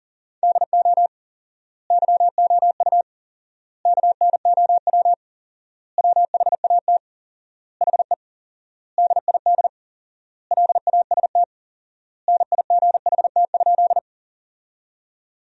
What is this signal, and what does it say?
Morse code, heard as a keyed tone, about 40 wpm, 700 Hz. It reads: DO YOU KNOW WHAT HE DID LAST NIGHT?
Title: 4x Word-Spacing - Sentences from Top 300 Words - 40wpm